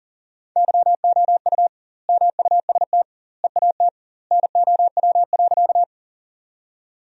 Morse code, heard as a keyed tone, about 40 wpm, 700 Hz. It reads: YOU MUST EAT NOW.
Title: 2x Word-Spacing - Sentences from Top 300 Words - 40wpm